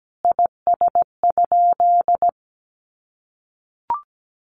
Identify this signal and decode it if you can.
Morse code, heard as a keyed tone, about 17 wpm, 700 Hz. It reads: IS?
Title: Sentences from Top 1500 Words 17wpm